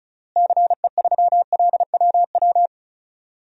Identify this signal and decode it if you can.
Morse code, heard as a keyed tone, about 36 wpm, 700 Hz. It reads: CE3LWW
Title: DXCC-100 Call Signs - 6 Characters - Repeated 3x - 35wpm